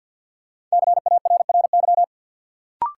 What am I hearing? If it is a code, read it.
Morse code, heard as a keyed tone, about 50 wpm, 700 Hz. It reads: CARRY
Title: Sentences from Top 400 Words - New Word Drill - 50wpm